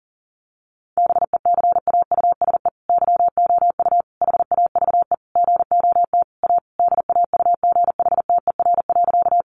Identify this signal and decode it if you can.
Morse code, heard as a keyed tone, about 40 wpm, 700 Hz. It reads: BECAUSE YOU HAVE GOT A DAUGHTER.
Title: Sentences from Top 1500 Words 40wpm